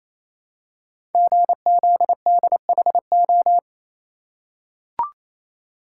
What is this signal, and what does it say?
Morse code, heard as a keyed tone, about 27 wpm, 700 Hz. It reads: GZDHO
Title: Code Groups - 5 Letters - 28wpm